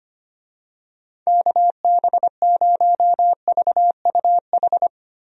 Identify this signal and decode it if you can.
Morse code, heard as a keyed tone, about 25 wpm, 700 Hz. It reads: KB0VUH
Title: USA Call Signs - 6 Characters - 25wpm